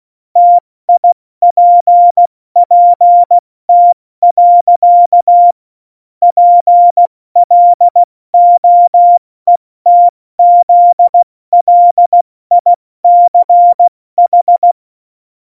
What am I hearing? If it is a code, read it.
Morse code, heard as a keyed tone, about 16 wpm, 700 Hz. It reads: TIPPT. PLOETZLICH
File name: Morsekurs / Witze / Witz 22. Januar 2019, 70 F80